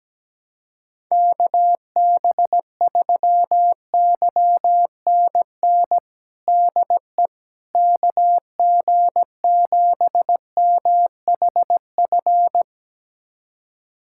Morse code, heard as a keyed tone, about 17 wpm, 700 Hz. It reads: KB3YNN DE KG7MHF
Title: USA Callsign DE Callsign - 6 Characters - 17wpm